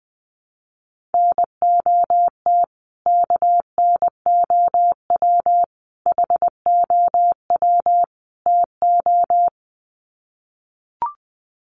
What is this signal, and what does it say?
Morse code, heard as a keyed tone, about 20 wpm, 700 Hz. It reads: NOT KNOW HOW TO
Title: Sets of 4 Words 20wpm